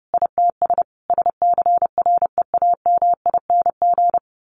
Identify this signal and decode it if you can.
Morse code, heard as a keyed tone, about 30 wpm, 700 Hz. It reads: ITS SCREAMING
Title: A Sound Of Thunder 30wpm